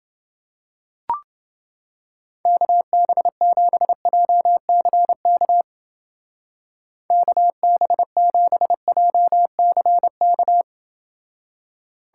Morse code, heard as a keyed tone, about 30 wpm, 700 Hz. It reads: KB7JCK KB7JCK